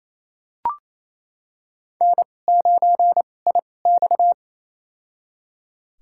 Morse code, heard as a keyed tone, about 28 wpm, spaced as for 20 wpm, 700 Hz. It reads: N9IX
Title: USA Call Signs - 4 Characters - 2x Character Spacing - 28wpm